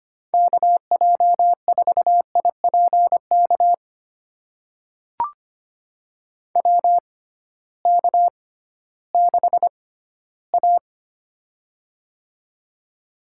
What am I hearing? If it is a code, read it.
Morse code, heard as a keyed tone, about 25 wpm, spaced as for 15 wpm, 700 Hz. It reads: KJ4IPK WK6A
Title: USA Call Signs - 6x Character Spacing - 25wpm